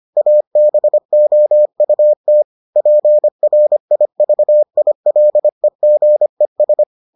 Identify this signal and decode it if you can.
Morse code, heard as a keyed tone, about 25 wpm, 600 Hz. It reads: ABOUT PRIVILEGES